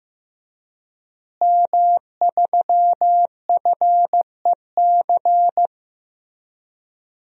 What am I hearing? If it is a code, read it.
Morse code, heard as a keyed tone, about 16 wpm, 700 Hz. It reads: M3FEC
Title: DXCC-100 Call Signs - 5 Characters - Repeated 2x - 15wpm